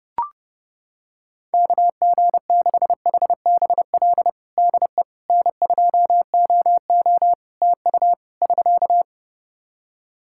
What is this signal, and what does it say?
Morse code, heard as a keyed tone, about 30 wpm, 700 Hz. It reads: KG6HBL DE N2OO TU <SK>
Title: K1USN SST QSO Practice 30wpm